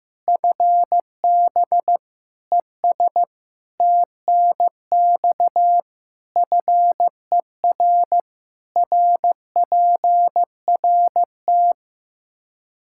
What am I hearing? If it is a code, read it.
Morse code, heard as a keyed tone, about 15 wpm, 700 Hz. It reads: FB ES TNX FER RPRT
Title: K1USN SST QSO Practice 15wpm